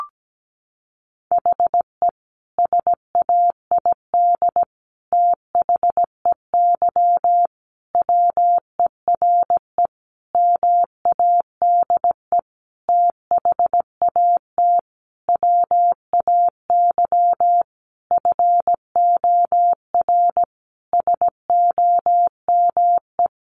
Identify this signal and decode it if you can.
Morse code, heard as a keyed tone, about 17 wpm, 700 Hz. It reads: HE SAID THEY WERE MADE THAT WAY FOR SOME
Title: Sentences from Top 700 Words 17wpm